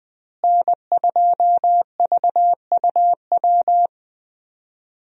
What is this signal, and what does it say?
Morse code, heard as a keyed tone, about 21 wpm, 700 Hz. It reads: N2VUW
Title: USA Call Signs - 5 Characters - Repeated 3x - 20wpm